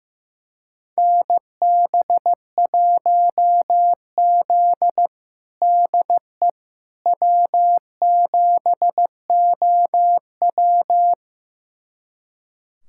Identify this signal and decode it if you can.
Morse code, heard as a keyed tone, about 16 wpm, 700 Hz. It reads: NB1Z DE W7OW
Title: USA Callsign DE Callsign - 4 Characters - 15wpm